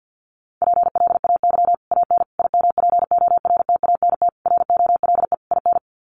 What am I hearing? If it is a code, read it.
Morse code, heard as a keyed tone, about 50 wpm, 700 Hz. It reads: PLAY AN IMPORTANT ROLE IN
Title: Sets of 5 Words Encore 50wpm